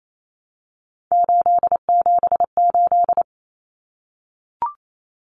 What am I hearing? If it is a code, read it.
Morse code, heard as a keyed tone, about 28 wpm, 700 Hz. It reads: 878
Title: Common Cut Numbers - 3-Digit 28wpm